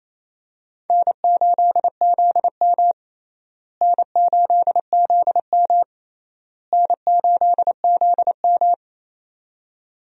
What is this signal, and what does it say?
Morse code, heard as a keyed tone, about 28 wpm, 700 Hz. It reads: N8ZM N8ZM N8ZM
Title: USA Call Signs - 4 Characters - Repeated 3x - 28wpm